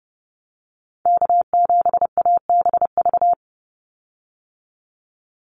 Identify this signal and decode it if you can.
Morse code, heard as a keyed tone, about 30 wpm, 700 Hz. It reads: K7ABV
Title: Call Signs Repeated 3x - 30wpm